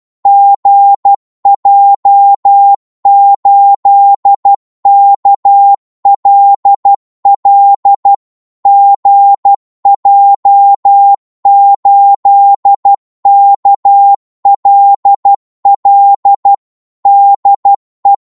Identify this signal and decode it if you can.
Morse code, heard as a keyed tone, about 12 wpm, 800 Hz. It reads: GJ8KLL GJ8KLL DE
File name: Morse / qso-12wpm